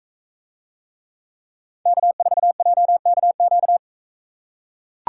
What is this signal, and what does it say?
Morse code, heard as a keyed tone, about 42 wpm, 700 Hz. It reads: K4JKQ